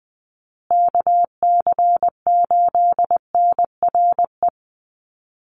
Morse code, heard as a keyed tone, about 20 wpm, 700 Hz. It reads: KC8NRE
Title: USA Call Signs Repeated 3x 20wpm